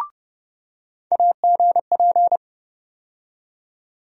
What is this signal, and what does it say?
Morse code, heard as a keyed tone, about 31 wpm, 700 Hz. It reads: AGP
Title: USA Call Sign Postfixes - 3 Letters 30wpm